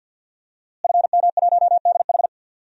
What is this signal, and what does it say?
Morse code, heard as a keyed tone, about 49 wpm, 700 Hz. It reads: FM1DH
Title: DXCC-100 Call Signs - 5 Characters - Repeated 2x - 50wpm